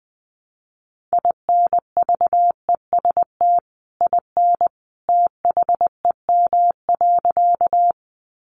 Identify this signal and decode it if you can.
Morse code, heard as a keyed tone, about 20 wpm, 700 Hz. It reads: INVEST IN THEM.
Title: Sentences from Top 1100 Words 20wpm